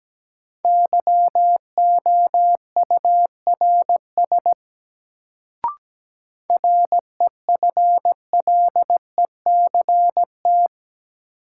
Morse code, heard as a keyed tone, about 17 wpm, 700 Hz. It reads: YOURS REFLECT
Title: Sentences from Top 1500 Words - New Word Drill - 17wpm